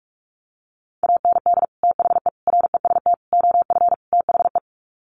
Morse code, heard as a keyed tone, about 44 wpm, 700 Hz. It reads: AND THE REST OF THE
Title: Sets of 5 Words 45wpm